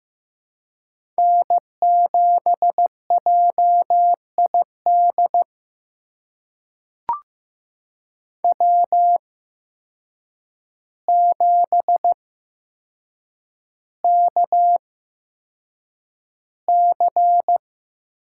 Morse code, heard as a keyed tone, about 15 wpm, spaced as for 7 wpm, 700 Hz. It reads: N7JID W7KC
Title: USA Call Signs - 5 Characters - 8x Character Spacing - 15wpm